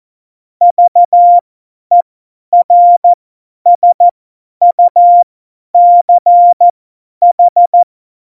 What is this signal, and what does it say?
Morse code, heard as a keyed tone, about 14 wpm, spaced as for 10 wpm, 700 Hz. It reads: VERSUCH
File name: Morsekurs / Witze / Witz 31. Oktober 17, Tempo 50